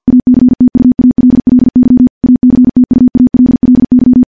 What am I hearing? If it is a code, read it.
Morse code, heard as a keyed tone, about 48 wpm, 250 Hz. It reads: ACTUALLY ACTUALLY
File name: morse-training-50WPM